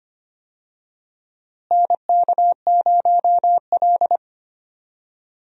Morse code, heard as a keyed tone, about 25 wpm, 700 Hz. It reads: NK0L
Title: USA Call Signs - 4 Characters - Repeated 3x - 25wpm